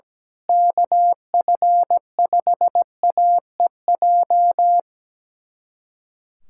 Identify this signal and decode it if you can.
Morse code, heard as a keyed tone, about 17 wpm, 700 Hz. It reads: KF5AEJ
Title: USA Callsign DE Callsign - 6 Characters - 17wpm